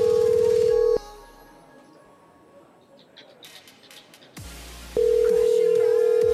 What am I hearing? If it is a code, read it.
Ringback tone (North America), 440 + 480 Hz.